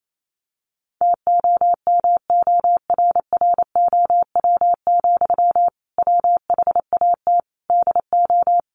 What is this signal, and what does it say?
Morse code, heard as a keyed tone, about 28 wpm, 700 Hz. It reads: TOMORROW, WHAT DO
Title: Sentences from Top 1500 Words 28wpm